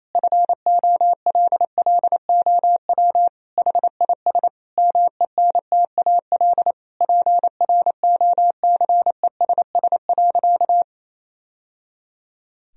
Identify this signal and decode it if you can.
Morse code, heard as a keyed tone, about 28 wpm, 700 Hz. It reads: FOLLOW HIS MENTAL PROCESS.
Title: Sentences from Top 1000 Words - Review Part 4 - 28wpm